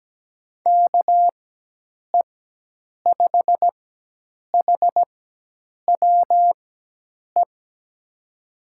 Morse code, heard as a keyed tone, about 17 wpm, spaced as for 8 wpm, 700 Hz. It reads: KE5HWE
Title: USA Call Signs - 6 Characters - 4x Character Spacing - 17wpm